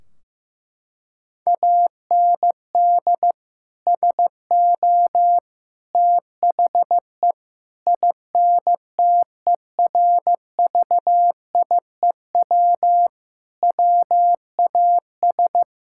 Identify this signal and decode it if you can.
Morse code, heard as a keyed tone, about 16 wpm, 700 Hz. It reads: AND SO THE INTERVIEW WAS
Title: Sentences from Top 1600 Words 15wpm